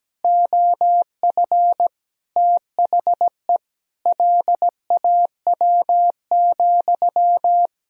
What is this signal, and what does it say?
Morse code, heard as a keyed tone, about 17 wpm, 700 Hz. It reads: OF THE LAW,
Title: Sentences from Top 1000 Words - Review Part 3 - 17wpm